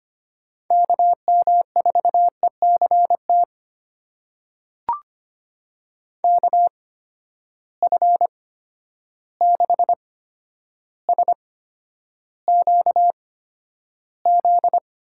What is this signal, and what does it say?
Morse code, heard as a keyed tone, about 25 wpm, spaced as for 11 wpm, 700 Hz. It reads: KM4ECT KF6SQZ
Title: USA Call Signs - 6 Characters - 8x Character Spacing - 25wpm